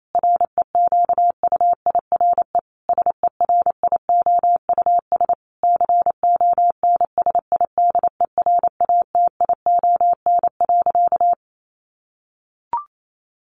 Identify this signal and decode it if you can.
Morse code, heard as a keyed tone, about 28 wpm, 700 Hz. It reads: REQUIRE SERIOUS CONSIDERATION.
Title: Sentences from Top 1500 Words 28wpm